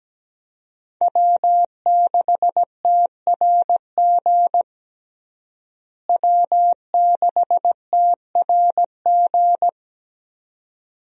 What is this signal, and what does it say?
Morse code, heard as a keyed tone, about 17 wpm, 700 Hz. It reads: W6TRG W6TRG
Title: USA Call Signs - 5 Characters - Repeated 2x - 17wpm